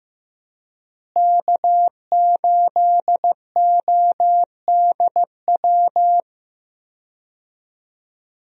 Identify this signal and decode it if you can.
Morse code, heard as a keyed tone, about 15 wpm, 700 Hz. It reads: K8ODW
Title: USA Call Signs - 5 Characters - Repeated 3x - 15wpm